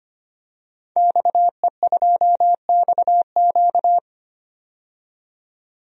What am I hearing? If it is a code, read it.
Morse code, heard as a keyed tone, about 25 wpm, 700 Hz. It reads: XE2XQ
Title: DXCC-100 Call Signs - 5 Characters - 25wpm